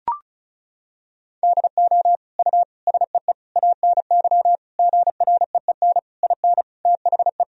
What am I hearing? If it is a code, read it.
Morse code, heard as a keyed tone, about 36 wpm, 700 Hz. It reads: DO U SEE ANY GREEN IN THE
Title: Sentences from Top 500 Words and Abbreviations 35wpm